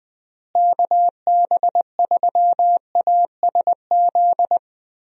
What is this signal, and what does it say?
Morse code, heard as a keyed tone, about 20 wpm, 700 Hz. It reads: KB3ASZ
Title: USA Call Signs - 6 Characters - Repeated 2x - 20wpm